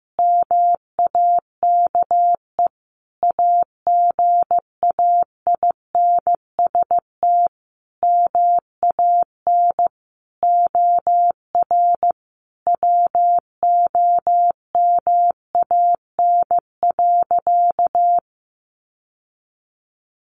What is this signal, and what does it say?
Morse code, heard as a keyed tone, about 15 wpm, 700 Hz. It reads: MAKE AGAINST MAN OR WOMAN.
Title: Sentences from Top 1400 Words 15wpm